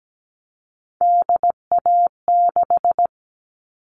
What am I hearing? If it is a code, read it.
Morse code, heard as a keyed tone, about 17 wpm, 700 Hz. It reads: DA6
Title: All Cut Numbers - 3-Digit 17wpm